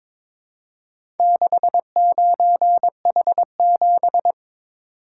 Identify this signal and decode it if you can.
Morse code, heard as a keyed tone, about 22 wpm, 700 Hz. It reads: 69H7
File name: practice / 22wpm / Four Letter-Number Combinations - 22wpm